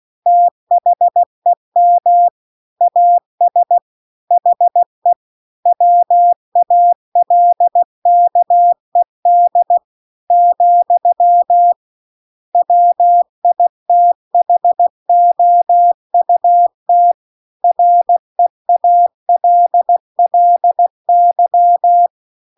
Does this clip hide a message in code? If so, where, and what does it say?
Yes: Morse code from 0.3 to 22.1 s, heard as a keyed tone, about 16 wpm, 700 Hz. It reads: THEM AS HE WALKED , WITHOUT REALLY